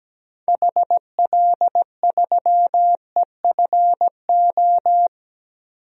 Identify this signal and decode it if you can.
Morse code, heard as a keyed tone, about 17 wpm, 700 Hz. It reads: HL3EFO
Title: DXCC-100 Call Signs - 6 Characters - 17wpm